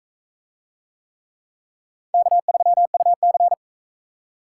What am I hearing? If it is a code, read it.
Morse code, heard as a keyed tone, about 41 wpm, 700 Hz. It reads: K3UC